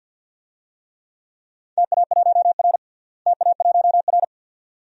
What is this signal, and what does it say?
Morse code, heard as a keyed tone, about 53 wpm, 700 Hz. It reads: TA1R TA1R
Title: DXCC-100 Call Signs - 4 Characters - Repeated 2x - 50wpm